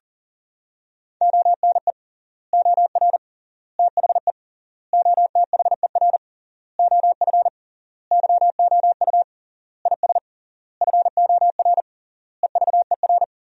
Morse code, heard as a keyed tone, about 40 wpm, 700 Hz. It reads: ONE OR THE OTHER OF YOU IS FOR EVER
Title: 3x Word-Spacing - Sentences from Top 400 Words - 40wpm